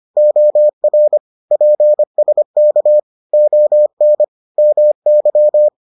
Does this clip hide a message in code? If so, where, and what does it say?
Yes: Morse code from 0.2 to 5.7 s, heard as a keyed tone, about 25 wpm, 600 Hz. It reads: OR PSK ON MY